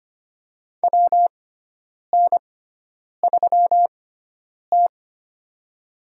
Morse code, heard as a keyed tone, about 25 wpm, spaced as for 9 wpm, 700 Hz. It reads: WN3T
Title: USA Call Signs - 4 Characters - 6x Character Spacing - 25wpm